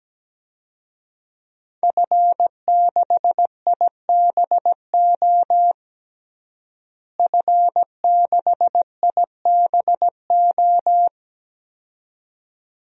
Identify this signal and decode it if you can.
Morse code, heard as a keyed tone, about 17 wpm, 700 Hz. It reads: F6IBO F6IBO
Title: DXCC-100 Call Signs - 5 Characters - Repeated 2x - 17wpm